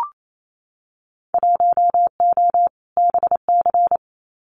Morse code, heard as a keyed tone, about 28 wpm, 700 Hz. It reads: 1O BC
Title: Winter Field Day Exchanges Encore 28wpm